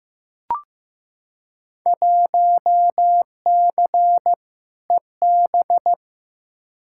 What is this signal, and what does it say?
Morse code, heard as a keyed tone, about 15 wpm, 700 Hz. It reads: 1C EB
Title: Field Day Exchanges Encore - 15wpm